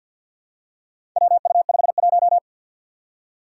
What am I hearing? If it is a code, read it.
Morse code, heard as a keyed tone, about 46 wpm, 700 Hz. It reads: WU51
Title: Four Challenging Letter-Number Combinations 50wpm